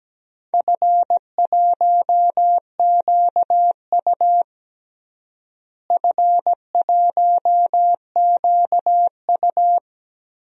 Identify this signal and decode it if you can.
Morse code, heard as a keyed tone, about 17 wpm, 700 Hz. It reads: F1QU F1QU
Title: DXCC-100 Call Signs - 4 Characters - Repeated 2x - 17wpm